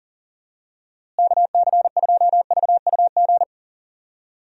Morse code, heard as a keyed tone, about 40 wpm, 700 Hz. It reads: KC2VUG
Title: USA Call Signs - 6 Characters - Repeated 3x - 40wpm